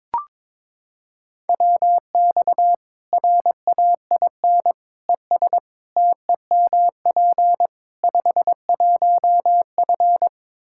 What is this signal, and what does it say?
Morse code, heard as a keyed tone, about 22 wpm, 700 Hz. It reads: WX RAIN ES TEMP 51F
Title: Practice QSO - Exchange 2 - Info - 22wpm